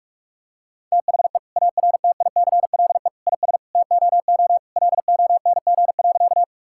Morse code, heard as a keyed tone, about 46 wpm, 700 Hz. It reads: THE ARTICLE IS TOO LONG.
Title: Sentences from Top 1000 Words - Review Part 5 - 45wpm